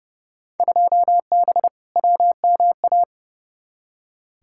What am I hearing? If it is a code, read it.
Morse code, heard as a keyed tone, about 30 wpm, 700 Hz. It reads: 2B WMA
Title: Field Day Exchanges Encore - 30wpm